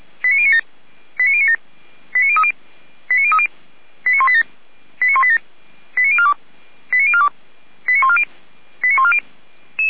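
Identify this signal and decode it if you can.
Five-tone selective call (ZVEI), heard as quick runs of five tones, tones of about 70 ms. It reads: 89097 89097 89030 89030 89187 89187 89042 89042 89140 89140
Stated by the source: ZVEI five-tone selective call received via SDR.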